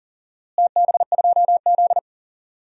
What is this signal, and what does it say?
Morse code, heard as a keyed tone, about 40 wpm, 700 Hz. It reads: TB2Z
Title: DXCC-100 Call Signs - 4 Characters - Repeated 2x - 40wpm